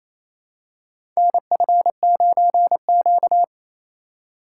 Morse code, heard as a keyed tone, about 28 wpm, 700 Hz. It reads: NF9Q
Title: USA Call Signs - 4x Character Spacing - 28wpm